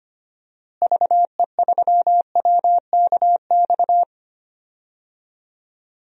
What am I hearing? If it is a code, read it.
Morse code, heard as a keyed tone, about 25 wpm, 700 Hz. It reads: VE3WKX
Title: DXCC-100 Call Signs - 6 Characters - Repeated 3x - 25wpm